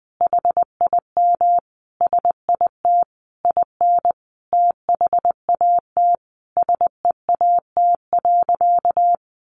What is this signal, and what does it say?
Morse code, heard as a keyed tone, about 20 wpm, 700 Hz. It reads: HIM SIT IN THAT SEAT.
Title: Sentences from Top 1500 Words 20wpm